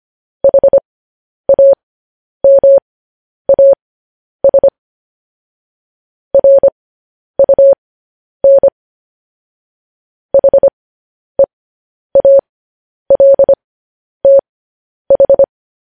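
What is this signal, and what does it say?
Morse code, heard as a keyed tone, about 25 wpm, spaced as for 10 wpm, 550 Hz. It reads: HAMAS RUN HEALTH
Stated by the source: Morse code news headlines at 10 wpm.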